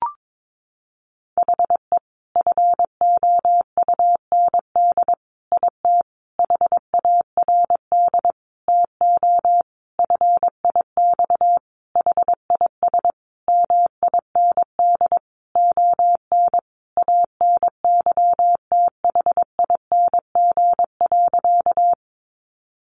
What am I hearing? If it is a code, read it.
Morse code, heard as a keyed tone, about 22 wpm, 700 Hz. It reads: HE FOUND IT HARD TO FIX HIS MIND ON ANYTHING.
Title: Sentences from Top 1400 Words 22wpm